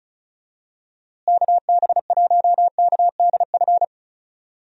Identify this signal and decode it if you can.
Morse code, heard as a keyed tone, about 34 wpm, 700 Hz. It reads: KB1KDF
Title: USA Call Signs - 6 Characters - Repeated 3x - 35wpm